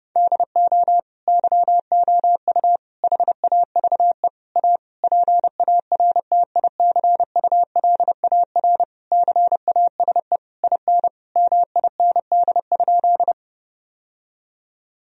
Morse code, heard as a keyed tone, about 30 wpm, 700 Hz. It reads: DO YOU HAVE A PARTICULAR CASE IN MIND?
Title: Sentences from Top 1000 Words - Review Part 4 - 30wpm